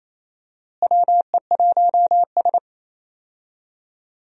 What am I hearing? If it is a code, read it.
Morse code, heard as a keyed tone, about 28 wpm, 700 Hz. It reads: WE1S
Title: USA Call Signs - 4 Characters - Repeated 2x - 28wpm